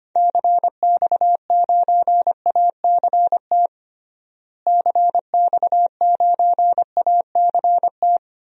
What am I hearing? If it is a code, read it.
Morse code, heard as a keyed tone, about 25 wpm, 700 Hz. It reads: CX9ACT CX9ACT
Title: DXCC-100 Call Signs - 6 Characters - Repeated 2x - 25wpm